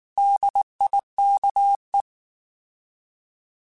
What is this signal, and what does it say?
Morse code, heard as a keyed tone, about 19 wpm, 800 Hz. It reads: DIKE